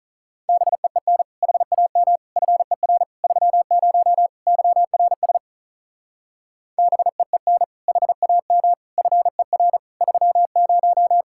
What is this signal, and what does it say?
Morse code, heard as a keyed tone, about 44 wpm, 700 Hz. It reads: BEEN HAM FER 30 YRS BEEN HAM FER 30